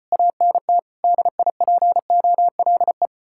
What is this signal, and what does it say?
Morse code, heard as a keyed tone, about 32 wpm, 700 Hz. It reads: ANT DIPOLE